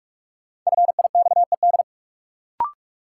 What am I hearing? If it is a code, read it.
Morse code, heard as a keyed tone, about 45 wpm, 700 Hz. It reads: FIXED
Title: Sentences from Top 1100 Words - New Word Drill - 45wpm